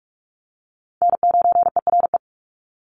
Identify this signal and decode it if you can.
Morse code, heard as a keyed tone, about 44 wpm, 700 Hz. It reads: N9ERE
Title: USA Call Signs 45wpm